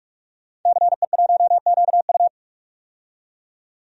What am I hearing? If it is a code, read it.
Morse code, heard as a keyed tone, about 46 wpm, 700 Hz. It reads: CE1QU
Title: DXCC-100 Call Signs - 5 Characters - 45wpm